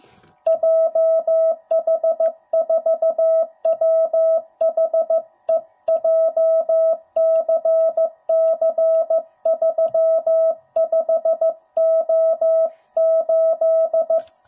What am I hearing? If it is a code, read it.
Morse code, heard as a keyed tone, about 15 wpm, 650 Hz. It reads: JH4WHE JCC35O8
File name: jh8cbh / 2021cw / cw05b